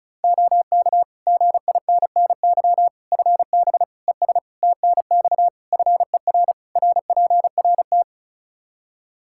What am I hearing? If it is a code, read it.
Morse code, heard as a keyed tone, about 36 wpm, 700 Hz. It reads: OK GINNY FB ES TNX FER RPRT
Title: Practice QSO - Exchange 2 - Info - 35wpm